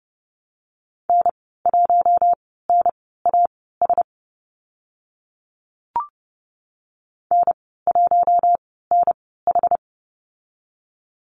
Morse code, heard as a keyed tone, about 30 wpm, spaced as for 17 wpm, 700 Hz. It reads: N1NAS N1NH
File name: Lesson 14g - Number 2 - Call Signs 30wpm